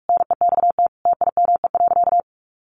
Morse code, heard as a keyed tone, about 44 wpm, 700 Hz. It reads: NEXT TIME.
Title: Sentences from Top 1500 Words 45wpm